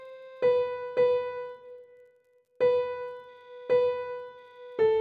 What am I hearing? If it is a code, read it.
There is no code.